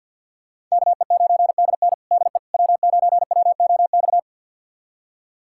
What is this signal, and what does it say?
Morse code, heard as a keyed tone, about 52 wpm, 700 Hz. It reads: KE9DN DE W9WOX
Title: USA Callsign DE Callsign - 5 Characters - 50wpm